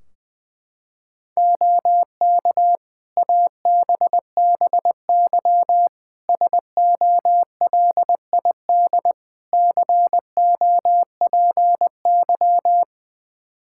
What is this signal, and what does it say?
Morse code, heard as a keyed tone, about 20 wpm, 700 Hz. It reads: OK ABBY SOLID COPY
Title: Practice QSO - Exchange 3 - More Info - 20wpm